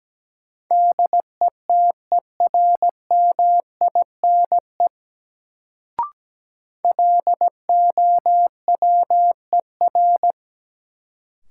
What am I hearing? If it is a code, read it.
Morse code, heard as a keyed tone, about 17 wpm, 700 Hz. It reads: DETERMINE LOWER
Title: Sentences from Top 700 Words - New Word Drill - 17wpm